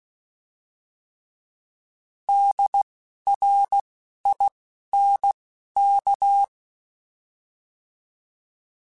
Morse code, heard as a keyed tone, about 16 wpm, spaced as for 12 wpm, 800 Hz. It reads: DRINK
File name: Alphabetical / Alphabetical mp3s / Ds @10